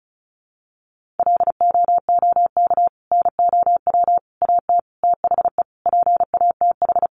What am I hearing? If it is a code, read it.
Morse code, heard as a keyed tone, about 34 wpm, 700 Hz. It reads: LOOK NOW AT THE PATH
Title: Sentences from Top 1400 Words 35wpm